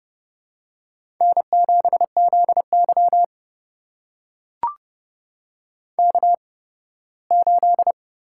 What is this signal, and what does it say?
Morse code, heard as a keyed tone, about 30 wpm, spaced as for 18 wpm, 700 Hz. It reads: N7ZY K8
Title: USA Call Signs - 8x Character Spacing - 30wpm